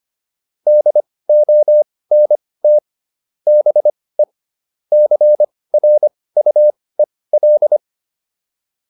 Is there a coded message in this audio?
Yes: Morse code from 0.7 to 7.8 s, heard as a keyed tone, about 25 wpm, spaced as for 18 wpm, 600 Hz. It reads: DONT BE CRUEL